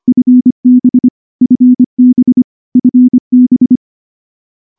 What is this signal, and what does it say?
Morse code, heard as a keyed tone, about 23 wpm, 250 Hz. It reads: FB FB FB
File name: morse / morse-training-25WPM